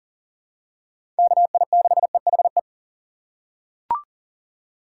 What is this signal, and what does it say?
Morse code, heard as a keyed tone, about 40 wpm, 700 Hz. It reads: KI6EHE
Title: USA Call Signs - 6 Characters - 2x Character Spacing - 40wpm